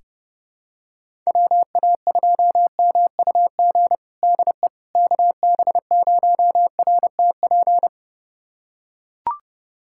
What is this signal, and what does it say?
Morse code, heard as a keyed tone, about 32 wpm, 700 Hz. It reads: WA2MUG DE KB0RTP
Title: USA Callsign DE Callsign - 6 Characters - 30wpm